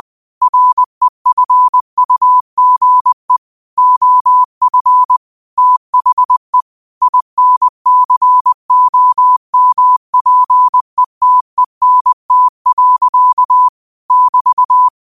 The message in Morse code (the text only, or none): REFUGE OF THE INCOMPETENT. <BT>